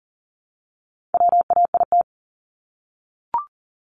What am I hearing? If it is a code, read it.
Morse code, heard as a keyed tone, about 40 wpm, 700 Hz. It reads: WAIT
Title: Top 500 Words Repeated 40wpm